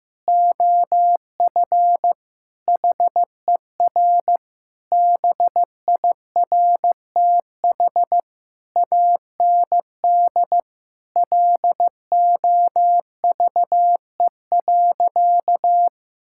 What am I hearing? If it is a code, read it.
Morse code, heard as a keyed tone, about 15 wpm, 700 Hz. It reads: OF HER BIRTH AND LOVE.
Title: Sentences from Top 1000 Words - Review Part 5 - 15wpm